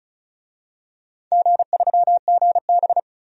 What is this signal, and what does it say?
Morse code, heard as a keyed tone, about 36 wpm, 700 Hz. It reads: G3GB
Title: DXCC-100 Call Signs - 4 Characters - 35wpm